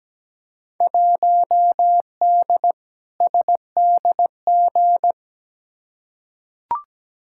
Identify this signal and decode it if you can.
Morse code, heard as a keyed tone, about 17 wpm, 700 Hz. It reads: 1D SDG
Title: Field Day Exchanges - 17wpm